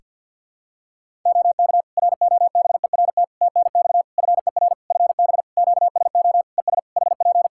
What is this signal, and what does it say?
Morse code, heard as a keyed tone, about 50 wpm, 700 Hz. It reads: OK ROBERT TNX FER FB QSO ES HP
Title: K1USN SST QSO Practice 50wpm